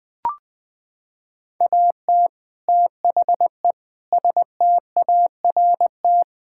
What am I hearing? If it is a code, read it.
Morse code, heard as a keyed tone, about 20 wpm, 700 Hz. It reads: AT THE START
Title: Sets of 6 Words - 20wpm